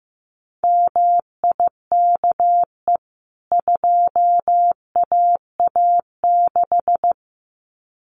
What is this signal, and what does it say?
Morse code, heard as a keyed tone, about 15 wpm, 700 Hz. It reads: MIKE 2AA6
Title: CWT Exchange 15wpm